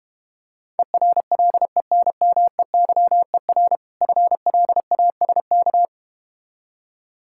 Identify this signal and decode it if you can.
Morse code, heard as a keyed tone, about 31 wpm, 690 Hz. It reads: ERLENMEYER FLASK
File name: practice / 25wpm / Difficult Words - Speed-Racing with Hints - 25wpm